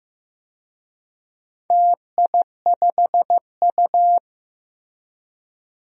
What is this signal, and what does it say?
Morse code, heard as a keyed tone, about 15 wpm, 700 Hz. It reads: TI5U
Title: DXCC-100 Call Signs - 4 Characters - Repeated 3x - 15wpm